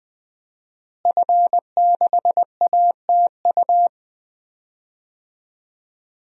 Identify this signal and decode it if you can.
Morse code, heard as a keyed tone, about 21 wpm, 700 Hz. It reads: F6ATU
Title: DXCC-100 Call Signs - 5 Characters - Repeated 3x - 20wpm